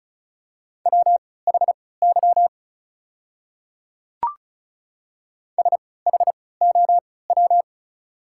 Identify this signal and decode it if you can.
Morse code, heard as a keyed tone, about 34 wpm, spaced as for 20 wpm, 690 Hz. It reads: WHY SHOW
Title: 3x Character Spacing - Top 200 Words - 35wpm